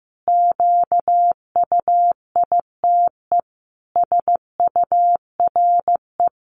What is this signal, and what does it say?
Morse code, heard as a keyed tone, about 15 wpm, 700 Hz. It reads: QUITE SURE
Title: Sentences from Top 700 Words 15wpm